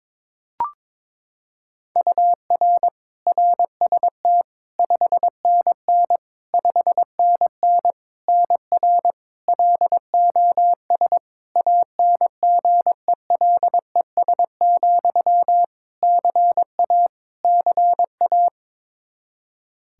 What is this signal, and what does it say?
Morse code, heard as a keyed tone, about 22 wpm, 700 Hz. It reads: UR RST 5NN 5NN NR LOS ANGELES, CA CA
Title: K1USN SST QSO Practice 22wpm